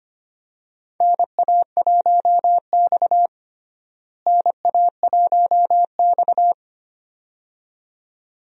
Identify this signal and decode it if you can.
Morse code, heard as a keyed tone, about 25 wpm, 700 Hz. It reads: NA1X NA1X
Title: USA Call Signs - 4 Characters - Repeated 2x - 25wpm